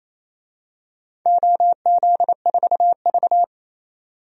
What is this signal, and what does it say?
Morse code, heard as a keyed tone, about 28 wpm, 700 Hz. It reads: OZ4V
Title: DXCC-100 Call Signs - 4 Characters - 28wpm